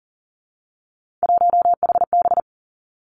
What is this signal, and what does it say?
Morse code, heard as a keyed tone, about 42 wpm, 700 Hz. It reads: 1HB
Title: Three Challenging Letter-Number Combinations 40wpm